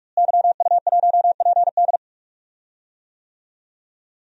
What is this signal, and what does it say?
Morse code, heard as a keyed tone, about 48 wpm, 700 Hz. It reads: YU1PD
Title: DXCC-100 Call Signs - 5 Characters - Repeated 2x - 45wpm